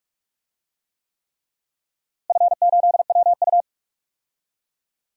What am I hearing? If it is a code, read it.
Morse code, heard as a keyed tone, about 41 wpm, 700 Hz. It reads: F8WU
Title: DXCC-100 Call Signs - 4 Characters - Repeated 3x - 45wpm